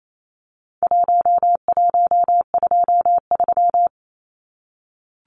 Morse code, heard as a keyed tone, about 28 wpm, 700 Hz. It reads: 1123
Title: Numbers - 4-Digit 28wpm